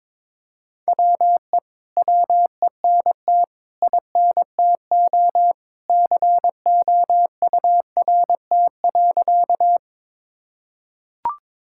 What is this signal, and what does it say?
Morse code, heard as a keyed tone, about 22 wpm, 700 Hz. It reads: WE WENT INTO COURT.
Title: Sentences from Top 1600 Words 22wpm